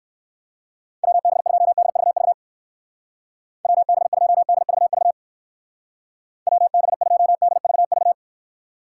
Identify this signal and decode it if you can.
Morse code, heard as a keyed tone, about 59 wpm, 700 Hz. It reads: WB2DVV WB2DVV WB2DVV